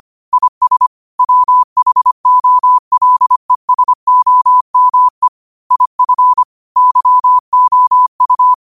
Morse code, heard as a keyed tone, about 25 wpm, 1000 Hz. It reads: IS WHOLESOME IF YOU